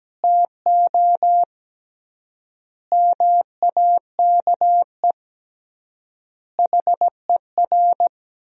Morse code, heard as a keyed tone, about 17 wpm, 700 Hz. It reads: TO MAKE HER
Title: 3x Word-Spacing - Sentences from Top 400 Words - 17wpm